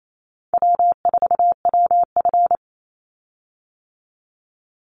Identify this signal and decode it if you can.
Morse code, heard as a keyed tone, about 28 wpm, 700 Hz. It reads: W4WF
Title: Call Signs - 28wpm